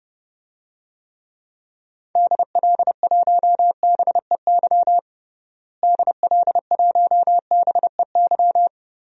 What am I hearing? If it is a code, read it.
Morse code, heard as a keyed tone, about 32 wpm, 700 Hz. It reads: DL1BEY DL1BEY
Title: DXCC-100 Call Signs - 6 Characters - Repeated 2x - 30wpm